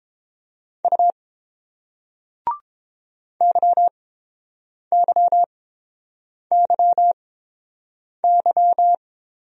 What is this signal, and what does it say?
Morse code, heard as a keyed tone, about 30 wpm, spaced as for 7 wpm, 700 Hz. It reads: U YYYY